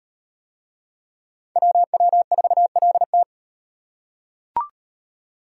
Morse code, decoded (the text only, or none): WW4LT